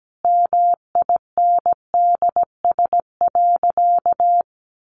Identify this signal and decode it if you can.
Morse code, heard as a keyed tone, about 17 wpm, 700 Hz. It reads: MINDS.
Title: Easy Phrases 17wpm